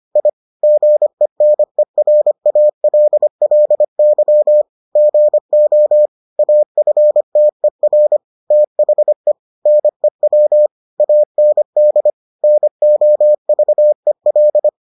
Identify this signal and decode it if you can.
Morse code, heard as a keyed tone, about 25 wpm, 600 Hz. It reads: I GENERALLY GO AFTER THE NEW AND NOVEL